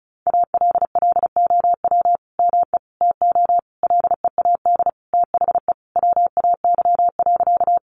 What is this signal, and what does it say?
Morse code, heard as a keyed tone, about 34 wpm, 700 Hz. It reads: ALLOW ME TO LEAD THE WAY.
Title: Sentences from Top 700 Words 35wpm